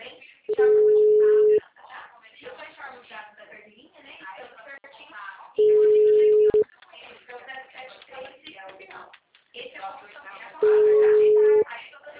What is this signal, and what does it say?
Ringback tone (Europe, 425 Hz), with speech.